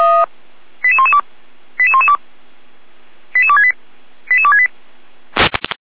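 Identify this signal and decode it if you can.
Five-tone selective call (ZVEI), heard as quick runs of five tones, tones of about 70 ms. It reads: 88112 88112 88268 88268
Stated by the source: ZVEI five-tone call 88199 received by SDR, fire brigade Feuerwehr Orlamünde.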